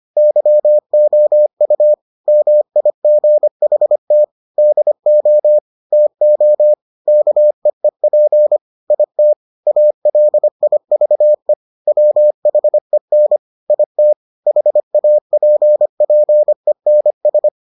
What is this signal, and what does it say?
Morse code, heard as a keyed tone, about 25 wpm, 600 Hz. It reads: YOU MIGHT DO TO KEEP IT ALIVE WHEN IT HAPPENS